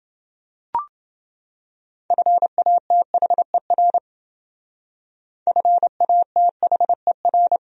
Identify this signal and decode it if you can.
Morse code, heard as a keyed tone, about 28 wpm, 700 Hz. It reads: FATHER FATHER